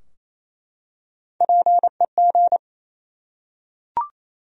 Morse code, heard as a keyed tone, about 28 wpm, 700 Hz. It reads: PEG
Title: USA Common Names - Max 3 Letters - 28wpm